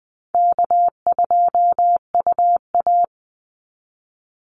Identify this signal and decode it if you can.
Morse code, heard as a keyed tone, about 20 wpm, 700 Hz. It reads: K2UA
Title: Call Signs Repeated 2x - 20wpm